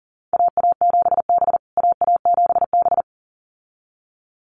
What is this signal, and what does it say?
Morse code, heard as a keyed tone, about 37 wpm, 700 Hz. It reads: AA7B AA7B
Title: USA Call Signs Repeated 2x 40wpm